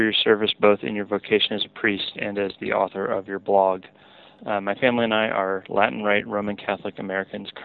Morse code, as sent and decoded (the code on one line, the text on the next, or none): none